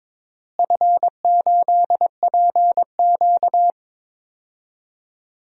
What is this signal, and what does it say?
Morse code, heard as a keyed tone, about 22 wpm, 700 Hz. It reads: F8PQ
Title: DXCC-100 Call Signs - 4 Characters - Repeated 3x - 22wpm